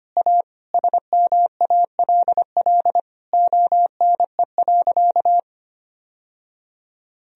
Morse code, heard as a keyed tone, about 25 wpm, 700 Hz. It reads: A SMALL ONE.
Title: Sentences from Top 1000 Words - Review Part 1 - 25wpm